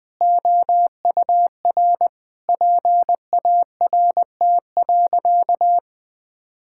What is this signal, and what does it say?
Morse code, heard as a keyed tone, about 20 wpm, 700 Hz. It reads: OUR PART.
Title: Sentences from Top 1000 Words - Review Part 6 - 20wpm